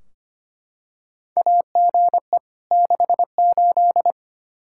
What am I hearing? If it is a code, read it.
Morse code, heard as a keyed tone, about 25 wpm, 700 Hz. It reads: AGE 68
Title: Practice QSO - Exchange 3 - More Info - 25wpm